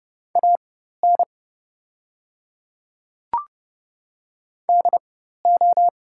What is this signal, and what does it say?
Morse code, heard as a keyed tone, about 30 wpm, spaced as for 14 wpm, 700 Hz. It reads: AN DO
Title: 4x Character Spacing - Top 200 Words - 30wpm